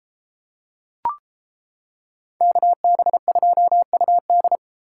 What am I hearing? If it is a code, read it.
Morse code, heard as a keyed tone, about 30 wpm, 700 Hz. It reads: KB2UD